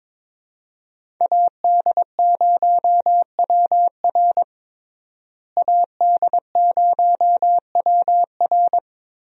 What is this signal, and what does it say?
Morse code, heard as a keyed tone, about 22 wpm, 700 Hz. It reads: AD0WR AD0WR
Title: USA Call Signs - 5 Characters - Repeated 3x - 22wpm